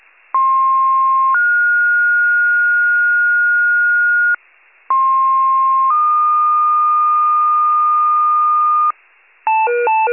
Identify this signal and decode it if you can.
Two-tone page, heard as a one-second tone, then a three-second tone, about 1060 then 1430 Hz, about 1060 then 1220 Hz, with noise. It is heard from 0.3 to 8.9 s.